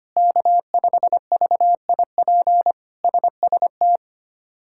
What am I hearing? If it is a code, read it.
Morse code, heard as a keyed tone, about 25 wpm, 700 Hz. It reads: K5VIP SST
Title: K1USN SST QSO Practice - 25wpm